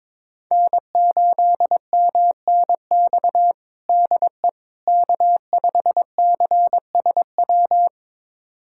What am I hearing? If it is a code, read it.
Morse code, heard as a keyed tone, about 22 wpm, 700 Hz. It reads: N8MNX DE K5CSW
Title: USA Callsign DE Callsign 22wpm